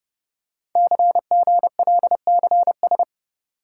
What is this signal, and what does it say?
Morse code, heard as a keyed tone, about 30 wpm, 700 Hz. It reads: CGLCS
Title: Code Groups - 5 Letters - 30wpm